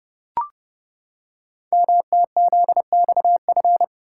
Morse code, heard as a keyed tone, about 30 wpm, 700 Hz. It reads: MTZXF
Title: Code Groups - 5 Letters - 30wpm